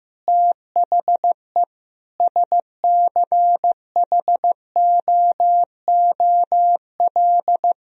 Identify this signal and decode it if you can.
Morse code, heard as a keyed tone, about 15 wpm, 700 Hz. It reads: THE SCHOOL
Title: Sentences from Top 1000 Words - Review Part 1 - 15wpm